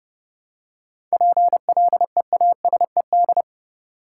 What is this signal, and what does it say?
Morse code, heard as a keyed tone, about 30 wpm, 700 Hz. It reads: PLEASED
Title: Seven Letter Words - 30wpm